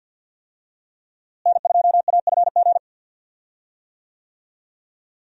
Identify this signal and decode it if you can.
Morse code, heard as a keyed tone, about 50 wpm, 700 Hz. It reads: N2AFG
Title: USA Call Signs - 5 Characters - Repeated 3x - 50wpm